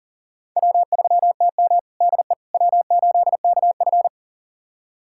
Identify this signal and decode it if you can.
Morse code, heard as a keyed tone, about 40 wpm, 700 Hz. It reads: W3TM DE W8KF
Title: USA Callsign DE Callsign - 4 Characters - 40wpm